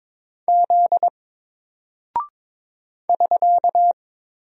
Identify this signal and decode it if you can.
Morse code, heard as a keyed tone, about 22 wpm, 700 Hz. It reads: Z <SK>
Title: Single Character (Letter:Number:Punctuation:Prosign) 22wpm